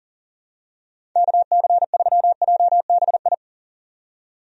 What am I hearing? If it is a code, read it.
Morse code, heard as a keyed tone, about 40 wpm, 700 Hz. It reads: KC3JBI
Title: USA Call Signs - 6 Characters - 40wpm